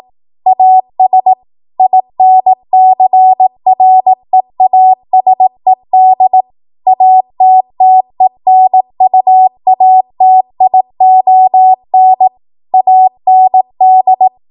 Morse code, heard as a keyed tone, about 18 wpm, 750 Hz. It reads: AS INCREASED ATTENUATION AND
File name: Morse / 18WPM